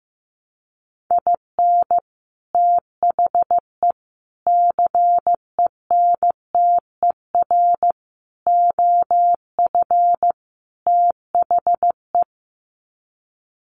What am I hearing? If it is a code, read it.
Morse code, heard as a keyed tone, about 15 wpm, 700 Hz. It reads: IN THE CENTER OF THE
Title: Sets of 5 Words 15wpm